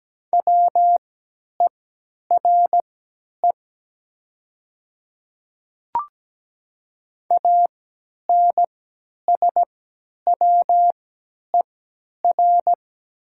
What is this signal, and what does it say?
Morse code, heard as a keyed tone, about 17 wpm, spaced as for 10 wpm, 690 Hz. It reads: WERE ANSWER
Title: 3x Character Spacing - Top 300 Words - 17wpm